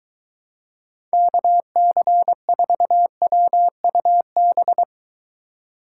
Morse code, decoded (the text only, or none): KC4WUB